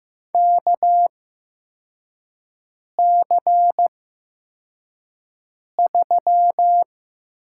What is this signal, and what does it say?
Morse code, heard as a keyed tone, about 15 wpm, spaced as for 4 wpm, 700 Hz. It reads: KC3
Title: USA Call Signs - 5 Characters - 8x Character Spacing - 15wpm